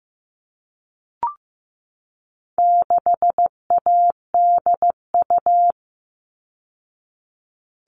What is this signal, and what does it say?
Morse code, heard as a keyed tone, about 15 wpm, 700 Hz. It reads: 6ADU
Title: All Cut Numbers - 4-Digit 15wpm